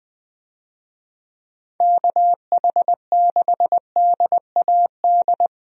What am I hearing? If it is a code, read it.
Morse code, heard as a keyed tone, about 20 wpm, 700 Hz. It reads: KH6DAD
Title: DXCC-100 Call Signs - 6 Characters - 20wpm